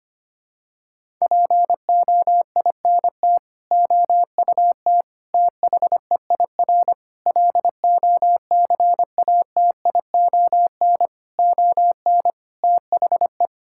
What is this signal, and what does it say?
Morse code, heard as a keyed tone, about 25 wpm, 700 Hz. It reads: POINT OUT THEIR LOCATION ON THE
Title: Sentences from Top 1000 Words - Review Part 6 - 25wpm